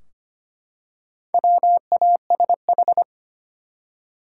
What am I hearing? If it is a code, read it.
Morse code, heard as a keyed tone, about 25 wpm, 700 Hz. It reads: WASH
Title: Four Letter Words - 25wpm